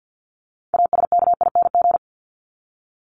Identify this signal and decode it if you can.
Morse code, heard as a keyed tone, about 50 wpm, 700 Hz. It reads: ASKING
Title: ING Suffix - Top 100 Words with Base Words 50wpm